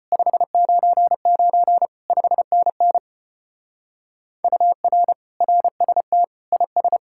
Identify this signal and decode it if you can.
Morse code, heard as a keyed tone, about 31 wpm, 700 Hz. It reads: 599 5NN UR RST IS
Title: Speed-Racing Common QSO Phrases Encore - 25wpm